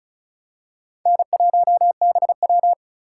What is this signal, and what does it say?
Morse code, heard as a keyed tone, about 36 wpm, 700 Hz. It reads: N1BW
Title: USA Call Signs - 2x Character Spacing - 35wpm